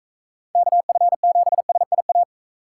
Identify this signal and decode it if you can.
Morse code, heard as a keyed tone, about 41 wpm, 700 Hz. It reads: KF7SIA